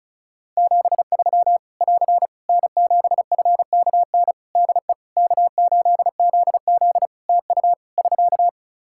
Morse code, heard as a keyed tone, about 34 wpm, 700 Hz. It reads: 73 <AR> N7FKN DE K8ZZ TU <SK>
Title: Practice QSO - Exchange 4 - Ending - 35wpm